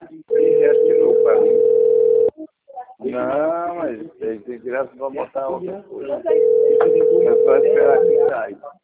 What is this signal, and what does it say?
Ringback tone (North America), 440 + 480 Hz, with speech.
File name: rg-1600-unknown- - - .35944